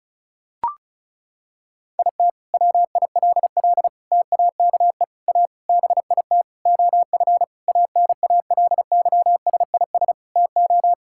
Morse code, heard as a keyed tone, about 34 wpm, 700 Hz. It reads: IT WILL TAKE A BIT OF ANALYSIS TO
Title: Sentences from Top 1000 Words - Review Part 5 - 35wpm